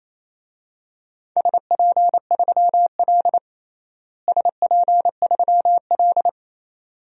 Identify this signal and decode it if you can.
Morse code, heard as a keyed tone, about 28 wpm, 700 Hz. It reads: SP3L SP3L
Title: DXCC-100 Call Signs - 4 Characters - Repeated 2x - 28wpm